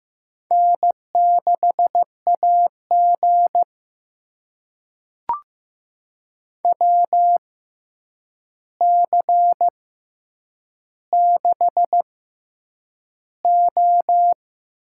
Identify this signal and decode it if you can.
Morse code, heard as a keyed tone, about 15 wpm, spaced as for 8 wpm, 700 Hz. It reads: N6AG WC6O
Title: USA Call Signs - 4 Characters - 6x Character Spacing - 15wpm